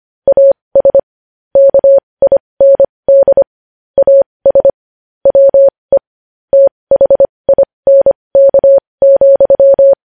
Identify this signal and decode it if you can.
Morse code, heard as a keyed tone, about 25 wpm, spaced as for 20 wpm, 550 Hz. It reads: AS KIND AS WE THINK,
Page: news-headlines-in-morse-code-at-20-wpm